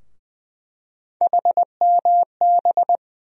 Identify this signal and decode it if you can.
Morse code, heard as a keyed tone, about 20 wpm, 700 Hz. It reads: HMB
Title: USA Call Sign Postfixes - 3 Letters 20wpm